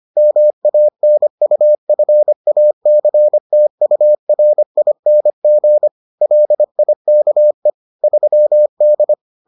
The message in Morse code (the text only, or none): MANUFACTURING LIKE 3D